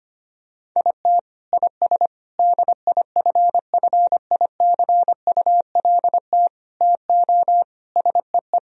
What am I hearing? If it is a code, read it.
Morse code, heard as a keyed tone, about 25 wpm, 700 Hz. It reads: IT IS DIFFICULT TO SEE
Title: Sentences from Top 1000 Words - Review Part 4 - 25wpm